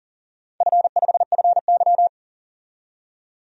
Morse code, heard as a keyed tone, about 40 wpm, 700 Hz. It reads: F5FY
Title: DXCC-100 Call Signs - 4 Characters - 40wpm